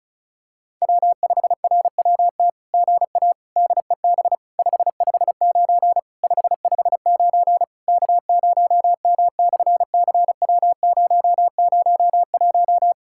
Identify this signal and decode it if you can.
Morse code, heard as a keyed tone, about 33 wpm, 700 Hz. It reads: W5RWT GA DEB 559 559 K0M/CW001
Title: SOTA QSO Practice 35wpm